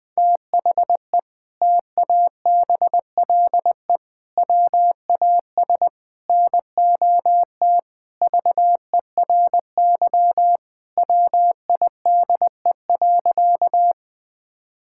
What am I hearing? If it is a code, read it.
Morse code, heard as a keyed tone, about 20 wpm, 700 Hz. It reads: THE TABLE WAS NOT VERY WIDE.
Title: Sentences from Top 1000 Words - Review Part 6 - 20wpm